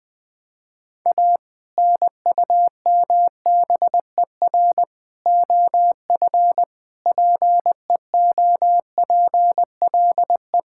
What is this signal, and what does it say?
Morse code, heard as a keyed tone, about 20 wpm, 700 Hz. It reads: A NUMBER OF PEOPLE
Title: Sentences with 2-Word Phrases - 20wpm